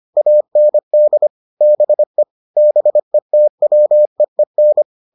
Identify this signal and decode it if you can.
Morse code, heard as a keyed tone, about 25 wpm, 600 Hz. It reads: AND BE BETWEEN